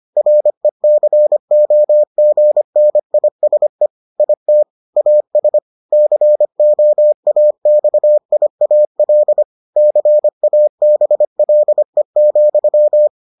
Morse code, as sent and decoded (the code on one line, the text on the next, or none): .-. -.-. --- --. -. .. ... / .. - / .- ... / -.-. --- .- -..- .. .- .-.. / -.-. .- -... .-.. --..--
RECOGNISE IT AS COAXIAL CABLE,